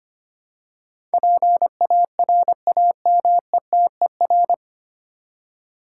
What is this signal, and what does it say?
Morse code, heard as a keyed tone, about 25 wpm, 700 Hz. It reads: PARAMETER
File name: practice / 25wpm / 9 Letter Words - First 3 Letters Given - 25wpm